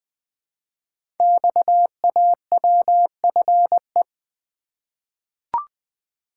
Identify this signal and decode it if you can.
Morse code, heard as a keyed tone, about 20 wpm, 700 Hz. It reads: XAWFE
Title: Code Groups - 5 Letters - 20wpm